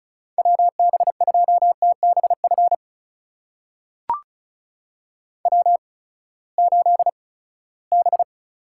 Morse code, heard as a keyed tone, about 36 wpm, spaced as for 20 wpm, 700 Hz. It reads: WB2TBF W8B
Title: USA Call Signs - 8x Character Spacing - 35wpm